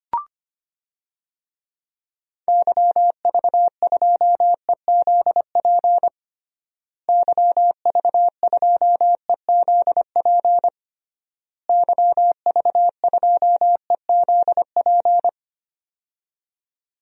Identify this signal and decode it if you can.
Morse code, heard as a keyed tone, about 25 wpm, 700 Hz. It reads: YV2EZP YV2EZP YV2EZP
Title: DXCC-100 Call Signs - 6 Characters - Repeated 3x - 25wpm